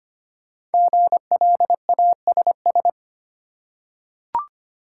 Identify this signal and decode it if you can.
Morse code, heard as a keyed tone, about 25 wpm, 690 Hz. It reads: GLASS
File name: practice / 25wpm / Easy Words with Hints - 25wpm